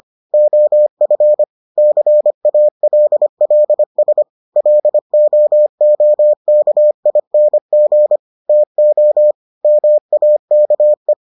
Morse code, heard as a keyed tone, about 25 wpm, 600 Hz. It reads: OF CALLS LOOKING TO MAKE